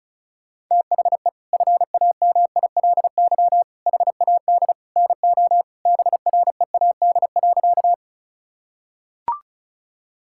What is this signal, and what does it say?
Morse code, heard as a keyed tone, about 36 wpm, 700 Hz. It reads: THE FAMILY HAD NO BREAD.
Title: Sentences from Top 1600 Words 35wpm